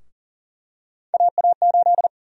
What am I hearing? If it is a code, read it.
Morse code, heard as a keyed tone, about 40 wpm, 700 Hz. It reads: AA8
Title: USA Call Sign Prefixes - 40wpm